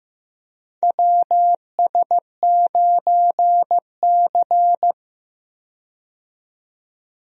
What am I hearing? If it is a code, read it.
Morse code, heard as a keyed tone, about 15 wpm, 700 Hz. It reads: WS9C
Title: USA Call Signs - 4 Characters - 15wpm